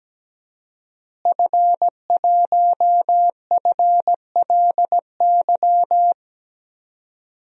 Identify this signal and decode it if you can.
Morse code, heard as a keyed tone, about 17 wpm, 700 Hz. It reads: F1FLY
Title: DXCC-100 Call Signs - 5 Characters - Repeated 3x - 17wpm